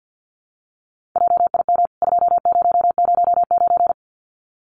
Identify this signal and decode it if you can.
Morse code, heard as a keyed tone, about 50 wpm, 700 Hz. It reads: JIM 2009
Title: CWT Exchange 50wpm